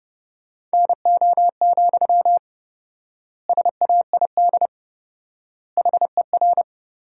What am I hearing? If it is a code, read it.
Morse code, heard as a keyed tone, about 32 wpm, 700 Hz. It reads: NO, SAID HER
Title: 4x Word-Spacing - Sentences from Top 300 Words - 30wpm